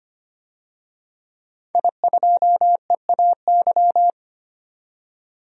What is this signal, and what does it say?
Morse code, heard as a keyed tone, about 25 wpm, 700 Hz. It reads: I2EAY
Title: DXCC-100 Call Signs - 5 Characters - 25wpm